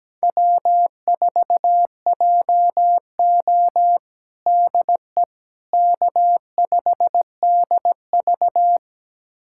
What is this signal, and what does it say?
Morse code, heard as a keyed tone, about 17 wpm, 700 Hz. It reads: W4JO DE K5DV
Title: USA Callsign DE Callsign - 4 Characters - 17wpm